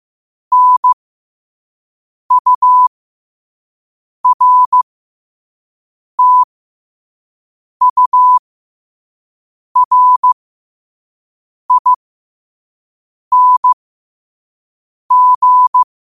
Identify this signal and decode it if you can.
Morse code, heard as a keyed tone, about 15 wpm, spaced as for 5 wpm, 1000 Hz. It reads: NURTURING
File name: qotdfiles / 5wpm